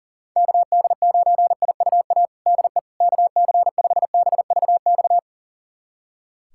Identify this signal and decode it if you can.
Morse code, heard as a keyed tone, about 40 wpm, 700 Hz. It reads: KD9IUA DE KC5BVX
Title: USA Callsign DE Callsign 40wpm